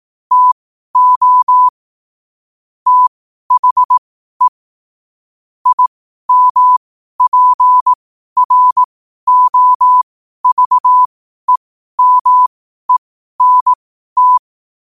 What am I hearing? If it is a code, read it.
Morse code, heard as a keyed tone, about 18 wpm, spaced as for 12 wpm, 1000 Hz. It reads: TO THE IMPROVEMENT
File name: qotdfiles / 12wpm